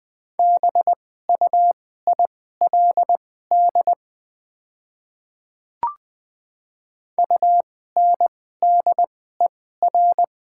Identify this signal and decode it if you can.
Morse code, heard as a keyed tone, about 20 wpm, spaced as for 14 wpm, 700 Hz. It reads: BUILD UNDER
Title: 2x Character Spacing - Top 200 Words - 20wpm